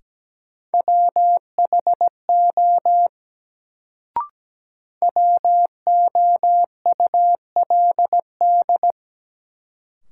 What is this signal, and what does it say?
Morse code, heard as a keyed tone, about 17 wpm, 700 Hz. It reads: WHO WOULD
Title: Sentences from Top 100 Words - New Word Drill - 17wpm